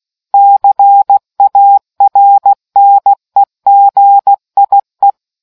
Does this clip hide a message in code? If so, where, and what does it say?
Yes: Morse code from 0.3 to 5.1 s, heard as a keyed tone, about 16 wpm, 800 Hz. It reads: CARNEGIE